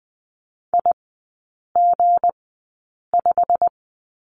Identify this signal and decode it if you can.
Morse code, heard as a keyed tone, about 20 wpm, spaced as for 8 wpm, 700 Hz. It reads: IG5
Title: 178 Lesson 29e - Slash - Three Characters 20wpm